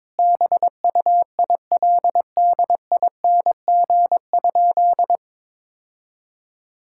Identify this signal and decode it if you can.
Morse code, heard as a keyed tone, about 22 wpm, 700 Hz. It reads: BUILDING?
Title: Sentences from Top 1000 Words - Review Part 5 - 22wpm